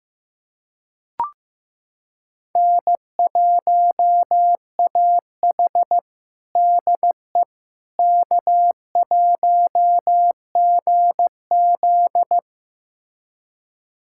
Morse code, heard as a keyed tone, about 15 wpm, 700 Hz. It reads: N1AH DE K1GZ
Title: USA Callsign DE Callsign - 4 Characters - 15wpm